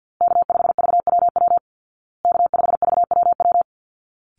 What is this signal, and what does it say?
Morse code, heard as a keyed tone, about 50 wpm, 700 Hz. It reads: K5VWW K5VWW
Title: Call Signs Repeated 2x - 50wpm